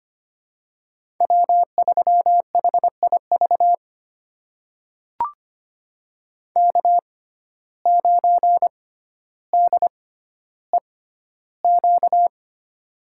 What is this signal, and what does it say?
Morse code, heard as a keyed tone, about 25 wpm, spaced as for 13 wpm, 700 Hz. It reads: W3HIV K9DEQ
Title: USA Call Signs - 5 Characters - 6x Character Spacing - 25wpm